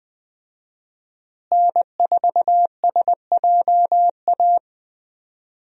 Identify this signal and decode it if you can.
Morse code, heard as a keyed tone, about 20 wpm, 700 Hz. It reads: N4SJA
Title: USA Call Signs - 5 Characters - Repeated 2x - 20wpm